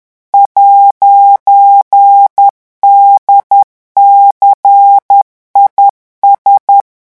Morse code, heard as a keyed tone, about 10 wpm, 800 Hz. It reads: 'DCIS
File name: morse mp3 / revision16-vitesse 10 mots